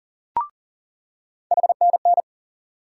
Morse code, heard as a keyed tone, about 40 wpm, 700 Hz. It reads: HNN
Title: USA Call Sign Postfixes - 3 Letters 40wpm